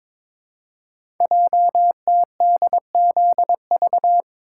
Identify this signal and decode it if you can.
Morse code, heard as a keyed tone, about 22 wpm, 700 Hz. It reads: JTDZV
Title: Code Groups - 5 Letters - 22wpm